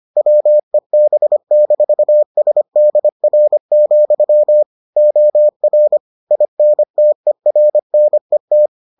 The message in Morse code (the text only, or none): WEB-SDR, OR INTERNET